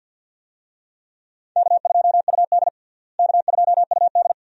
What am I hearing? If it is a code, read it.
Morse code, heard as a keyed tone, about 48 wpm, 700 Hz. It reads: K2UD K2UD
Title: USA Call Signs - 4 Characters - Repeated 2x - 50wpm